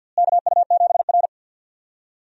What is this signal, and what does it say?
Morse code, heard as a keyed tone, about 52 wpm, 700 Hz. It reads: KU7R
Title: USA Call Signs - 4 Characters - Repeated 2x - 50wpm